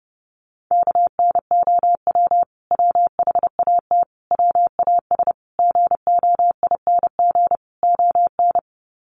Morse code, heard as a keyed tone, about 30 wpm, 700 Hz. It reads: KNOW WHAT WAS GOING ON
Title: Sets of 5 Words Encore2 30wpm